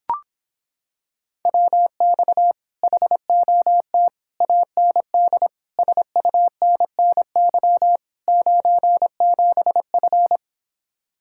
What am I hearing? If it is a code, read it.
Morse code, heard as a keyed tone, about 26 wpm, 700 Hz. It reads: WX HOT AND SUNNY 97F